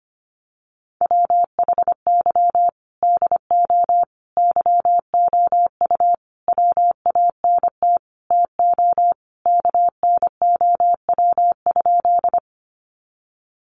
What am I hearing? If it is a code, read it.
Morse code, heard as a keyed tone, about 25 wpm, 700 Hz. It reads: WHY DO YOU WANT TO KNOW?
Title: Sentences from Top 200 Words 25wpm